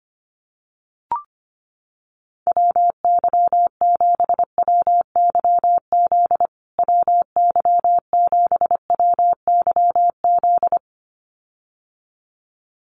Morse code, heard as a keyed tone, about 25 wpm, 700 Hz. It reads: WY7WYZ WY7WYZ
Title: USA Call Signs Repeated 2x 25wpm